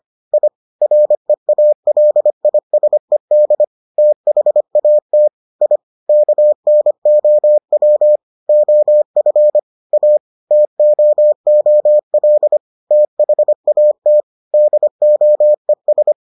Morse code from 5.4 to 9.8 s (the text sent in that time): I KNOW OF